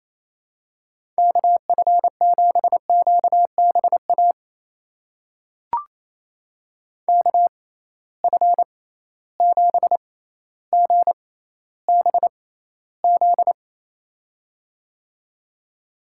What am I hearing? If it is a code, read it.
Morse code, heard as a keyed tone, about 28 wpm, spaced as for 14 wpm, 700 Hz. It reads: KF7QBA KF7GBZ
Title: USA Call Signs - 6 Characters - 6x Character Spacing - 28wpm